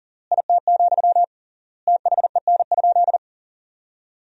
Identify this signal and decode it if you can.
Morse code, heard as a keyed tone, about 40 wpm, 700 Hz. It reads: IT, THEN?
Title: 3x Word-Spacing - Sentences from Top 200 Words - 40wpm